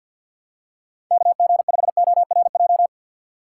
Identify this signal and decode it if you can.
Morse code, heard as a keyed tone, about 49 wpm, 700 Hz. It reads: KG5ORJ
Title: USA Call Signs - 6 Characters - 4x Character Spacing - 50wpm